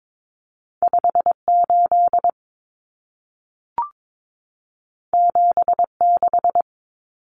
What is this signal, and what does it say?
Morse code, heard as a keyed tone, about 22 wpm, 700 Hz. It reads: 58 76
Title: Numbers - 2-Digit 22wpm